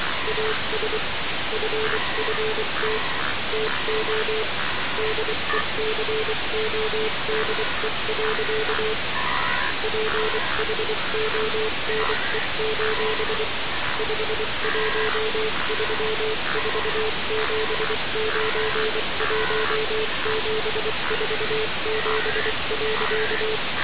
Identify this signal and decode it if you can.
Morse code, heard as a keyed tone, about 24 wpm, 440 Hz, with noise. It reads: AS FFT TO DECODE. PHONE: 5134791747.
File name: radiocap / REC z